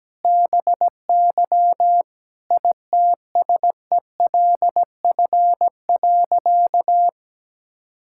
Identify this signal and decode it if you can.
Morse code, heard as a keyed tone, about 17 wpm, 700 Hz. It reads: BY ITSELF.
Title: Sentences from Top 1000 Words - Review Part 5 - 17wpm